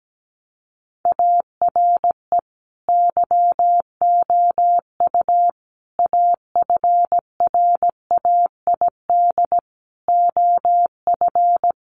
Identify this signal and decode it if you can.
Morse code, heard as a keyed tone, about 17 wpm, 700 Hz. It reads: ARE YOU AFRAID OF
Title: Sentences from Top 1500 Words 17wpm